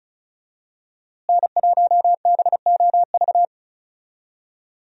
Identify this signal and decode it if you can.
Morse code, heard as a keyed tone, about 36 wpm, 700 Hz. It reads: N1BOV
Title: USA Call Signs - 5 Characters - Repeated 2x - 35wpm